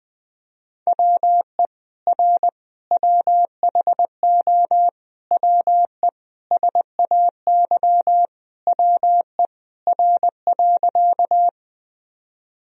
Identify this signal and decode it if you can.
Morse code, heard as a keyed tone, about 20 wpm, 700 Hz. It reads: WE R WHO WE SAY WE R.
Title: Sentences from Top 200 Words and Abbreviations 20wpm